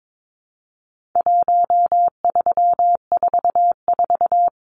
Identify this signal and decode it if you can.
Morse code, heard as a keyed tone, about 22 wpm, 700 Hz. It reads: 1344
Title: Numbers - 4-Digit 22wpm